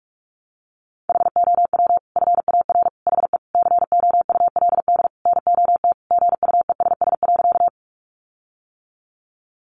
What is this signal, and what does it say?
Morse code, heard as a keyed tone, about 44 wpm, 700 Hz. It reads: HOW FAR HE COULD NOT GUESS.
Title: Sentences from Top 1500 Words 45wpm